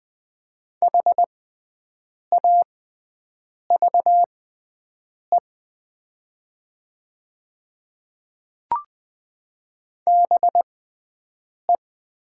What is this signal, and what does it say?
Morse code, heard as a keyed tone, about 20 wpm, spaced as for 7 wpm, 700 Hz. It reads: HAVE BE
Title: 6x Character Spacing - Top 100 Words - 20wpm